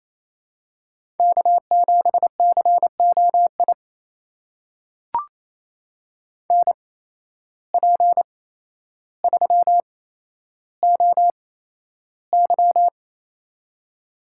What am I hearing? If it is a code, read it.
Morse code, heard as a keyed tone, about 28 wpm, spaced as for 12 wpm, 700 Hz. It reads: K7COI NP3OY
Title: USA Call Signs - 8x Character Spacing - 28wpm